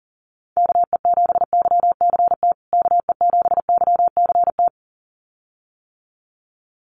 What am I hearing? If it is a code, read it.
Morse code, heard as a keyed tone, about 40 wpm, 700 Hz. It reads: KE7YCT KE7YCT
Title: USA Call Signs Repeated 2x 40wpm